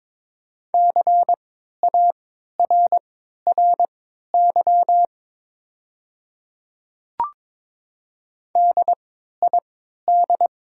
Morse code, heard as a keyed tone, about 22 wpm, spaced as for 12 wpm, 690 Hz. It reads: CARRY DID
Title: 3x Character Spacing - Top 500 Words - 22wpm